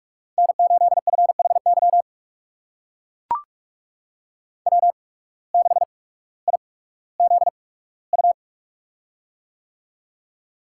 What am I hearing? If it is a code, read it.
Morse code, heard as a keyed tone, about 42 wpm, spaced as for 19 wpm, 700 Hz. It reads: N8FHY W6IZU
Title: USA Call Signs - 5 Characters - 8x Character Spacing - 45wpm